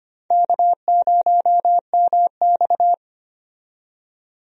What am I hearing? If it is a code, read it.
Morse code, heard as a keyed tone, about 25 wpm, 700 Hz. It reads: K0MX
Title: USA Call Signs - 4 Characters - 25wpm